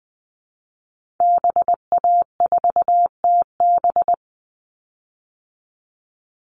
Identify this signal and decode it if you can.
Morse code, heard as a keyed tone, about 20 wpm, 700 Hz. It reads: BA4TB
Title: Call Signs Repeated 2x - 20wpm